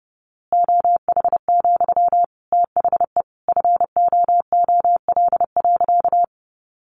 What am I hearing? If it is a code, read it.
Morse code, heard as a keyed tone, about 30 wpm, 700 Hz. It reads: OH, THE FOOL.
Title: A Sound Of Thunder 30wpm